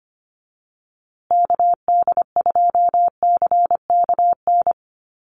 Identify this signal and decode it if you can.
Morse code, heard as a keyed tone, about 25 wpm, 700 Hz. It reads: KD2CKN
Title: USA Call Signs Repeated 2x 25wpm